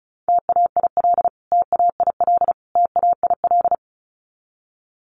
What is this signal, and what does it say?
Morse code, heard as a keyed tone, about 34 wpm, 700 Hz. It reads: TAIL TAIL TAIL
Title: Top 500 Words Repeated 35wpm